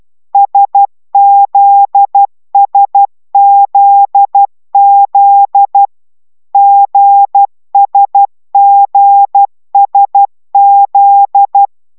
Morse code, heard as a keyed tone, about 12 wpm, 800 Hz. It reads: SZSZZ GSGSZ